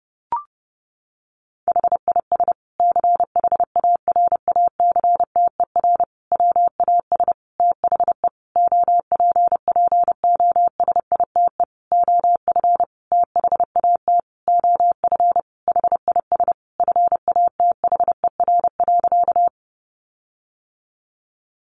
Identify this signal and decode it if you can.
Morse code, heard as a keyed tone, about 28 wpm, 700 Hz. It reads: HIS CHARACTER WAS THE OPPOSITE OF THAT OF HIS FATHER.
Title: Sentences from Top 1300 Words 30wpm